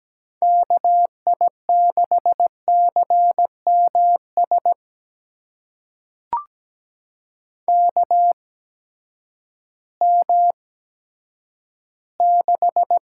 Morse code, heard as a keyed tone, about 17 wpm, spaced as for 10 wpm, 700 Hz. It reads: KI6CMS KM6
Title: USA Call Signs - 6 Characters - 8x Character Spacing - 17wpm